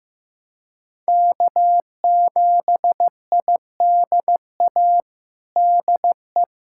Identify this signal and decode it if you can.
Morse code, heard as a keyed tone, about 15 wpm, 700 Hz. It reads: K7IDA DE
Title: USA Callsign DE Callsign - 5 Characters - 15wpm